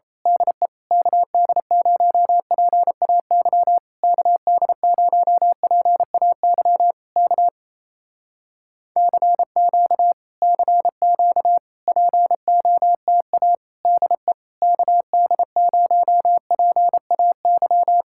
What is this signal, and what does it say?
Morse code, heard as a keyed tone, about 26 wpm, 700 Hz. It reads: DE KD0PAY KD0PAY K CQ CQ POTA DE KD0PAY
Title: Speed-Racing Common QSO Phrases Encore - 28wpm